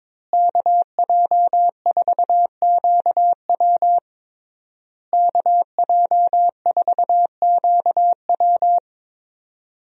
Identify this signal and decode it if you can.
Morse code, heard as a keyed tone, about 22 wpm, 700 Hz. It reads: KJ4QW KJ4QW
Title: USA Call Signs - 5 Characters - Repeated 3x - 22wpm